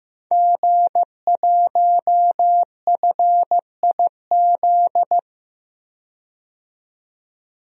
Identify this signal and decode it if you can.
Morse code, heard as a keyed tone, about 15 wpm, 700 Hz. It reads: G1FIZ
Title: DXCC-100 Call Signs - 5 Characters - 15wpm